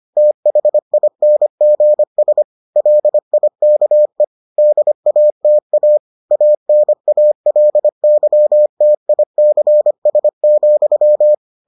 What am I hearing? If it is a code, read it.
Morse code, heard as a keyed tone, about 25 wpm, 600 Hz. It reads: THINGS LIKE DATA ANALYTICS,